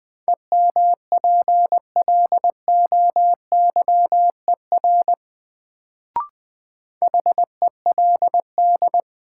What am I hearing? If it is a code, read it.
Morse code, heard as a keyed tone, about 20 wpm, 700 Hz. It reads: EMPLOYER HELD
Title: Sentences from Top 1600 Words - New Word Drill - 20wpm